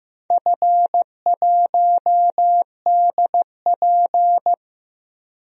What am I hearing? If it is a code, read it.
Morse code, heard as a keyed tone, about 15 wpm, 700 Hz. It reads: F1DP
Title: DXCC-100 Call Signs - 4 Characters - Repeated 3x - 15wpm